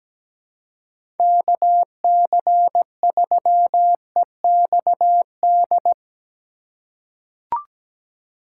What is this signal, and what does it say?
Morse code, heard as a keyed tone, about 17 wpm, 700 Hz. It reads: KC3EXD
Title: USA Call Signs - 4x Character Spacing - 17wpm